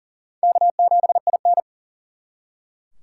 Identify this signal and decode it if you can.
Morse code, heard as a keyed tone, about 49 wpm, 700 Hz. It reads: K7IN
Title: Speed Racing - USA 1x2 Call Signs 40wpm